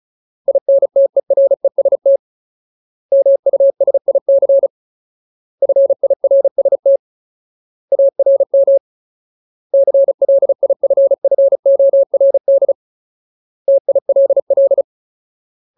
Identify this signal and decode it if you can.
Morse code, heard as a keyed tone, about 36 wpm, 550 Hz. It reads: INTEREST MUSIC FIRST ARM CLIFFORD TILL
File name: Ultimate Word List / 35WPM Ultimate